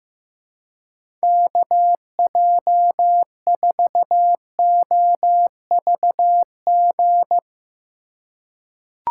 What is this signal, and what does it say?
Morse code, heard as a keyed tone, about 15 wpm, 700 Hz. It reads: KJ4OVG
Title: USA Call Signs - 2x Character Spacing - 15wpm